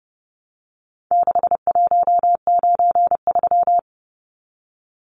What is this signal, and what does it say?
Morse code, heard as a keyed tone, about 30 wpm, 700 Hz. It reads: 6193
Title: Numbers - 4-Digit 30wpm